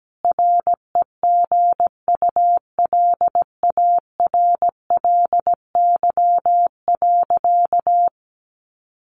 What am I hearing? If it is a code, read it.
Morse code, heard as a keyed tone, about 17 wpm, 700 Hz. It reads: REGULARLY.
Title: Sentences from Top 1400 Words 17wpm